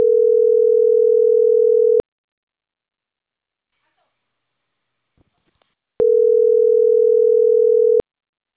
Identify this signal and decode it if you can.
Ringback tone (North America), 440 + 480 Hz.